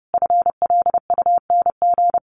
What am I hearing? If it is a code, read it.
Morse code, heard as a keyed tone, about 30 wpm, 700 Hz. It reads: FLUNG
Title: A Sound Of Thunder 30wpm